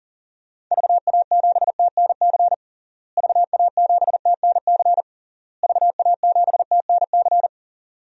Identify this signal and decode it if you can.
Morse code, heard as a keyed tone, about 37 wpm, 700 Hz. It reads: VA7TNC VA7TNC VA7TNC
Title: DXCC-100 Call Signs - 6 Characters - Repeated 3x - 40wpm